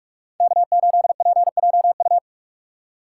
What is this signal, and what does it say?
Morse code, heard as a keyed tone, about 46 wpm, 700 Hz. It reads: K8PJU